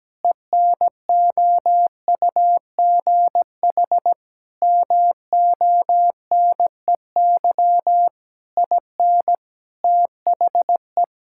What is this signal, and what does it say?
Morse code, heard as a keyed tone, about 17 wpm, 700 Hz. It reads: ENOUGH MONEY IN THE
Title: Sentences from Top 1000 Words - Review Part 6 - 17wpm